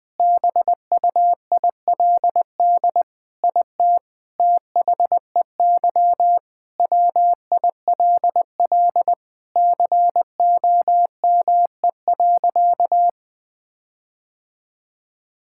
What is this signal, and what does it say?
Morse code, heard as a keyed tone, about 21 wpm, 700 Hz. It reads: BUILD IT THEY WILL COME.
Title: Sentences with 7-Word Phrases - Spoken Beginning - 20wpm